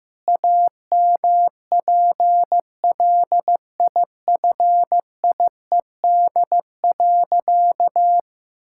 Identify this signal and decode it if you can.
Morse code, heard as a keyed tone, about 15 wpm, 700 Hz. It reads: AMPLIFIED.
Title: Pangram Challenge v2 - 15wpm